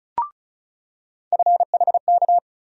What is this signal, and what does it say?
Morse code, heard as a keyed tone, about 36 wpm, 700 Hz. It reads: FHK
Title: USA Call Sign Postfixes - 3 Letters 35wpm